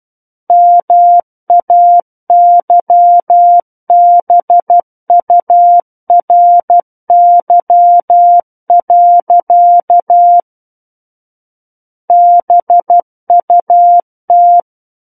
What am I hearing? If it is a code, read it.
Morse code, heard as a keyed tone, about 12 wpm, 700 Hz. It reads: MAYBURY. BUT
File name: Book One / 03-Chapter 3 - 12 WPM